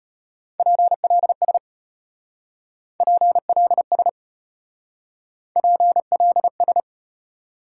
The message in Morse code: PLS PLS PLS

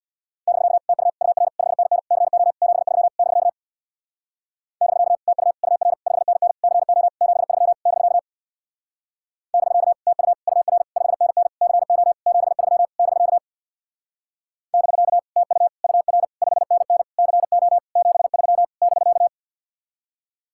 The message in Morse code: <BK> TU UR 5NN CO 73 <BK> <BK> TU UR 5NN CO 73 <BK> <BK> TU UR 5NN CO 73 <BK> <BK> TU UR 5NN CO 73 <BK>

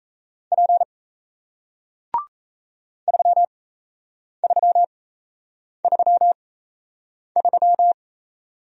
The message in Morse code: P 3333

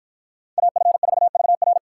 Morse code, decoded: AF4VR

